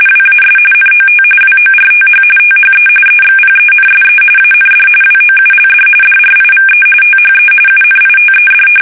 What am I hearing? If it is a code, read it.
RTTY, about 45.45 baud. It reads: WAS NOT WORTH THE EXPENSE AND WOULD DRIVE THE UNITED
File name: fsk850 / usnavy1